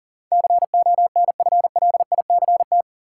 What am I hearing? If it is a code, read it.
Morse code, heard as a keyed tone, about 37 wpm, 700 Hz. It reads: CONFLICT
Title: Sentences from Top 1300 Words - New Word Drill - 40wpm